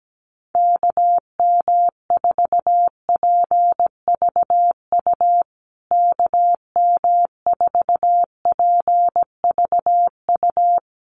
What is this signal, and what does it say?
Morse code, heard as a keyed tone, about 17 wpm, 700 Hz. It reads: KM4PVU KM4PVU
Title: USA Call Signs Repeated 3x 17wpm